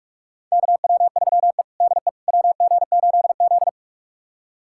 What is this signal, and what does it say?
Morse code, heard as a keyed tone, about 46 wpm, 700 Hz. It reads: KW3E DE WG8Z
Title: USA Callsign DE Callsign - 4 Characters - 45wpm